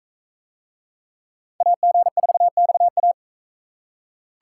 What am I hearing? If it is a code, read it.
Morse code, heard as a keyed tone, about 44 wpm, 700 Hz. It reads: AG4XA